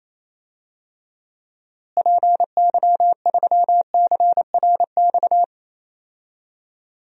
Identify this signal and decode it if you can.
Morse code, heard as a keyed tone, about 28 wpm, 700 Hz. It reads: PY3CRX
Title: DXCC-100 Call Signs - 6 Characters - Repeated 3x - 28wpm